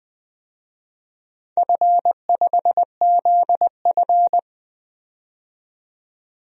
Morse code, heard as a keyed tone, about 20 wpm, 700 Hz. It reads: F5ZF
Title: DXCC-100 Call Signs - 4 Characters - 20wpm